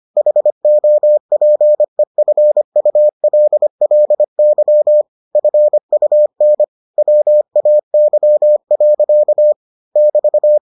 Morse code, heard as a keyed tone, about 25 wpm, 600 Hz. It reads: HOPEFULLY FUN WAY. <BT>